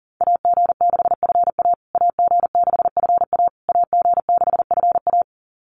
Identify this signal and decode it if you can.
Morse code, heard as a keyed tone, about 40 wpm, 700 Hz. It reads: AG6FA AG6FA AG6FA
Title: USA Call Signs Repeated 3x 40wpm